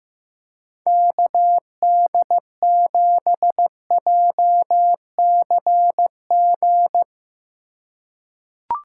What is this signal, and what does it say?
Morse code, heard as a keyed tone, about 15 wpm, 700 Hz. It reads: KD7JCG
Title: USA Call Signs - 2x Character Spacing - 15wpm